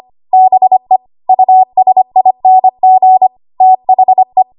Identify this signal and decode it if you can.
Morse code, heard as a keyed tone, about 25 wpm, 750 Hz. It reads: BE USING THE
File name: Archive / 25 WPM / 25WPM